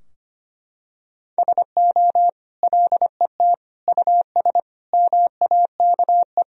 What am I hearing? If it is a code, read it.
Morse code, heard as a keyed tone, about 25 wpm, 700 Hz. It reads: SO LET US MAKE
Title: Sentences from Top 1000 Words - Review Part 4 - 25wpm